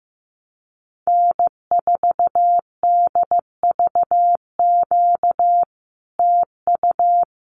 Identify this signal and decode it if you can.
Morse code, heard as a keyed tone, about 15 wpm, 700 Hz. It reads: N4DVQ TU
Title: Common QSO Phrases Encore 15wpm